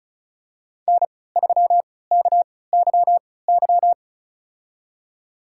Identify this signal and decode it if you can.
Morse code, heard as a keyed tone, about 36 wpm, spaced as for 20 wpm, 690 Hz. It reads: N3KYY
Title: USA Call Signs - 5 Characters - 3x Character Spacing - 35wpm